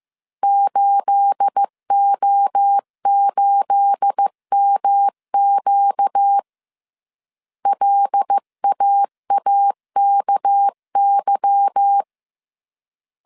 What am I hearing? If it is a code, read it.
Morse code, heard as a keyed tone, about 15 wpm, 800 Hz. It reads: 8O8MQ LAAKY
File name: CW / DET / 1 9 1500C S A 14wpm 3222